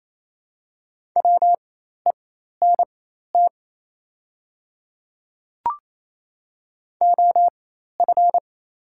Morse code, heard as a keyed tone, about 28 wpm, spaced as for 13 wpm, 700 Hz. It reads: WENT OF
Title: 4x Character Spacing - Top 200 Words - 28wpm